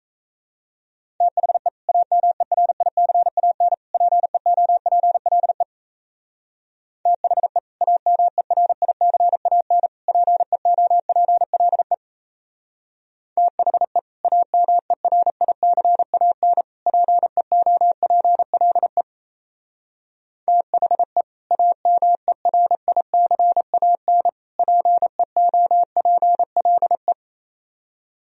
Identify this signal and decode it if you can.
Morse code, heard as a keyed tone, about 33 wpm, 700 Hz. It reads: THE AMERICAN PEOPLE THE AMERICAN PEOPLE THE AMERICAN PEOPLE THE AMERICAN PEOPLE
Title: Speed-Racing - Sets of 3 Words 28wpm